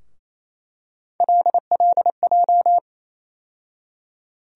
Morse code, heard as a keyed tone, about 28 wpm, 700 Hz. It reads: LLJ
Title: USA Call Sign Postfixes - 3 Letters 28wpm